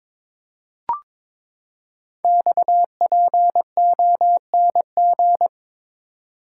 Morse code, heard as a keyed tone, about 22 wpm, 700 Hz. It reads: XPONG
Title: Code Groups - 5 Letters - 22wpm